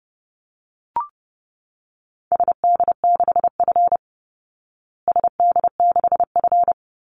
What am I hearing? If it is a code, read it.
Morse code, heard as a keyed tone, about 28 wpm, 700 Hz. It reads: SD6F SD6F
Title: Call Signs Repeated 2x - 30wpm